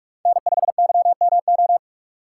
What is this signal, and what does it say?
Morse code, heard as a keyed tone, about 40 wpm, 700 Hz. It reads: N5YMO